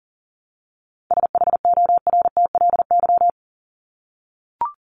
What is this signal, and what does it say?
Morse code, heard as a keyed tone, about 40 wpm, 700 Hz. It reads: SHORTLY
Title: LY Suffix - Top 100 Words with Base Words 40wpm